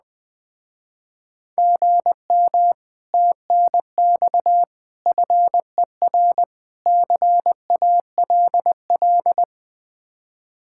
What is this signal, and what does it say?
Morse code, heard as a keyed tone, about 20 wpm, 700 Hz. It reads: GM TNX FER CALL
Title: Speed-Racing Common QSO Phrases - 17wpm